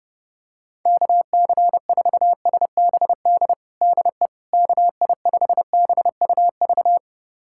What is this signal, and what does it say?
Morse code, heard as a keyed tone, about 29 wpm, 700 Hz. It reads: KC4SBD DE KI5BUV
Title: USA Callsign DE Callsign - 6 Characters - 30wpm